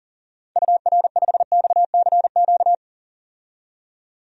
Morse code, heard as a keyed tone, about 37 wpm, 700 Hz. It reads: UR5XCQ
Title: DXCC-100 Call Signs - 6 Characters - 40wpm